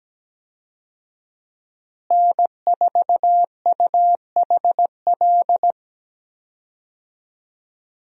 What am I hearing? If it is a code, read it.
Morse code, heard as a keyed tone, about 17 wpm, 700 Hz. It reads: N4UHL
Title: USA Call Signs - 5 Characters - Repeated 3x - 17wpm